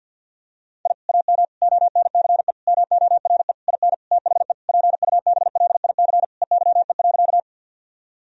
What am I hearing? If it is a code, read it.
Morse code, heard as a keyed tone, about 50 wpm, 700 Hz. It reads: I AM ONCE MORE IN THE PUBLIC EYE.
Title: Sentences from Top 1000 Words - Review Part 3 - 50wpm